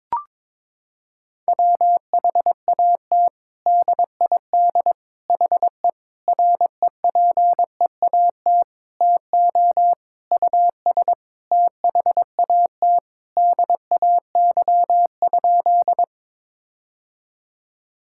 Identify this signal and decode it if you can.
Morse code, heard as a keyed tone, about 22 wpm, 700 Hz. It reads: WHAT DID HE REPEAT TO US THAT DAY?
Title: Sentences from Top 1600 Words 22wpm